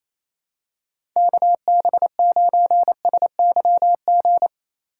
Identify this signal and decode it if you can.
Morse code, heard as a keyed tone, about 28 wpm, 700 Hz. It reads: KB9SYG
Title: USA Call Signs - 6 Characters - 6x Character Spacing - 28wpm